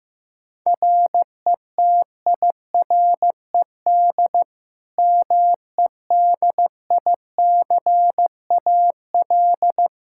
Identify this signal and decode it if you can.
Morse code, heard as a keyed tone, about 15 wpm, 700 Hz. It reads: RETIRED MEDICAL
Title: K1USN SST QSO Practice 15wpm